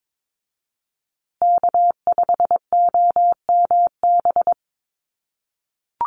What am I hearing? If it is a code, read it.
Morse code, heard as a keyed tone, about 22 wpm, 700 Hz. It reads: K5OMB
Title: USA Call Signs Repeated 2x 22wpm